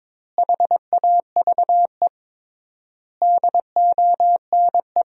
Morse code, heard as a keyed tone, about 22 wpm, 700 Hz. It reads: HAVE DONE
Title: 3x Word-Spacing - Sentences from Top 500 Words - 22wpm